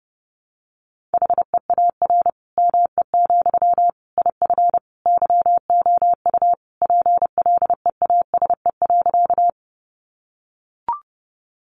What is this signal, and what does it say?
Morse code, heard as a keyed tone, about 28 wpm, 700 Hz. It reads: HEAR ME, IF YOU PLEASE.
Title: Sentences from Top 1500 Words 30wpm